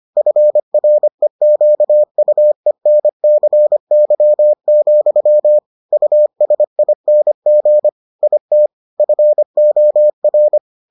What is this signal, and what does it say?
Morse code, heard as a keyed tone, about 25 wpm, 600 Hz. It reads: FREQUENCY, USING IT FOR